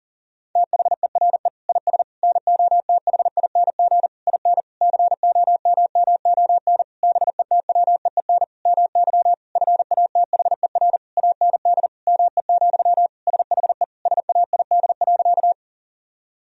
Morse code, heard as a keyed tone, about 37 wpm, 700 Hz. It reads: THERE IS NOTHING IN COMMON BETWEEN MY FATHER AND ME, SHE SAID.
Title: Sentences from Top 1000 Words - Review Part 3 - 40wpm